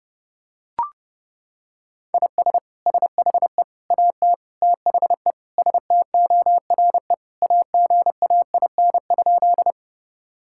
Morse code, heard as a keyed tone, about 32 wpm, 700 Hz. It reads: IS SHE AT THE STORE AGAIN?
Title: Sentences from Top 1000 Words - Review Part 5 - 30wpm